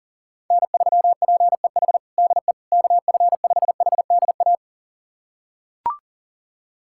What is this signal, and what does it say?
Morse code, heard as a keyed tone, about 37 wpm, 700 Hz. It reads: N3PEH DE KF5HDA
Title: USA Callsign DE Callsign 40wpm